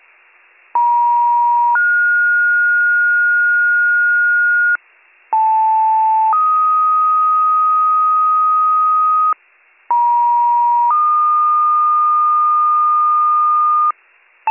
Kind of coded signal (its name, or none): two-tone page